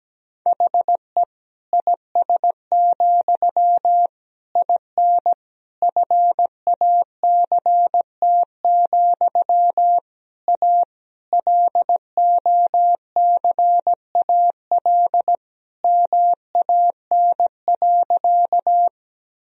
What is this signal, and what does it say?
Morse code, heard as a keyed tone, about 17 wpm, 700 Hz. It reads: HE IS, IN FACT, A LOCAL MAN.
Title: Sentences from Top 1000 Words - Review Part 3 - 17wpm